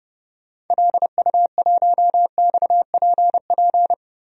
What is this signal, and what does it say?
Morse code, heard as a keyed tone, about 30 wpm, 700 Hz. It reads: LU1XPP
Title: DXCC-100 Call Signs - 6 Characters - 30wpm